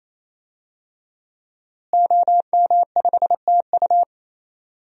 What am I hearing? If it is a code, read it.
Morse code, heard as a keyed tone, about 27 wpm, 700 Hz. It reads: OM5TU
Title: DXCC-100 Call Signs - 5 Characters - Repeated 3x - 28wpm